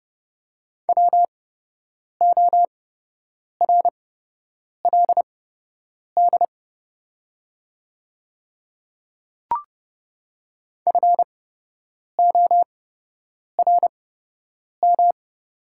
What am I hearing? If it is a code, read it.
Morse code, heard as a keyed tone, about 30 wpm, spaced as for 8 wpm, 700 Hz. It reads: WORLD FORM
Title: 8x Character Spacing - Top 200 Words - 30wpm